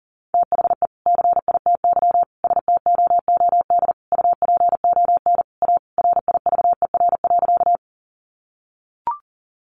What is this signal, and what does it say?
Morse code, heard as a keyed tone, about 37 wpm, 700 Hz. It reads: THE CITY STOOD UPON A RIVER.
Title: Sentences from Top 1100 Words 40wpm